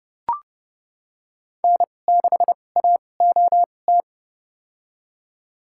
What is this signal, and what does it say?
Morse code, heard as a keyed tone, about 30 wpm, spaced as for 22 wpm, 700 Hz. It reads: N6AOT
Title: USA Call Signs - 5 Characters - 2x Character Spacing - 30wpm